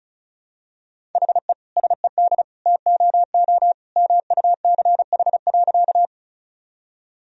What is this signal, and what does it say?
Morse code, heard as a keyed tone, about 36 wpm, 700 Hz. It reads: HE SED TOO MUCH.
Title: Sentences from Top 200 Words and Abbreviations 35wpm